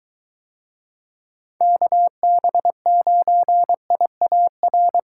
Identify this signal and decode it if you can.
Morse code, heard as a keyed tone, about 22 wpm, 700 Hz. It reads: KB9IAR